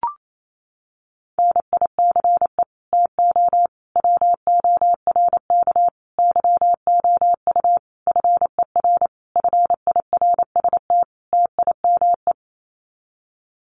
Morse code, heard as a keyed tone, about 28 wpm, 700 Hz. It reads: NICE TO WORK YOU FER FIRST TIME
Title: Common QSO Phrases 28wpm